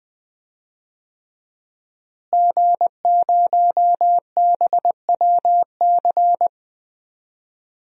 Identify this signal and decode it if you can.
Morse code, heard as a keyed tone, about 21 wpm, 700 Hz. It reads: G0BWC
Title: DXCC-100 Call Signs - 5 Characters - Repeated 3x - 20wpm